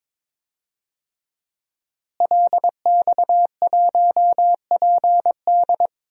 Morse code, heard as a keyed tone, about 22 wpm, 700 Hz. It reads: LX1PD
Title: DXCC-100 Call Signs - 5 Characters - Repeated 3x - 22wpm